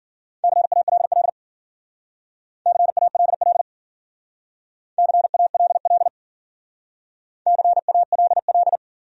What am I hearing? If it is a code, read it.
Morse code, heard as a keyed tone, about 52 wpm, 700 Hz. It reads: CALL CALL CALL CALL